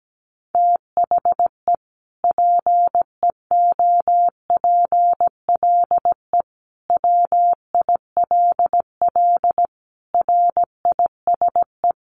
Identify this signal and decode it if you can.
Morse code, heard as a keyed tone, about 17 wpm, 700 Hz. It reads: THE PEOPLE WILL RISE
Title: Sentences from Top 900 Words 17wpm